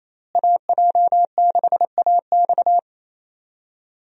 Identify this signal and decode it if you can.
Morse code, heard as a keyed tone, about 27 wpm, 700 Hz. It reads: AJ6AX
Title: USA Call Signs - 5 Characters - Repeated 3x - 28wpm